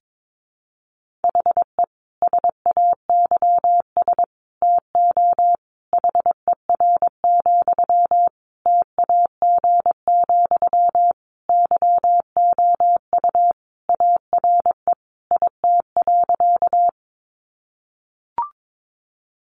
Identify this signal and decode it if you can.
Morse code, heard as a keyed tone, about 22 wpm, 700 Hz. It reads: HE SAYS TO HER, TAG, YOU ARE IT.
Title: Additional Easy Phrases 22wpm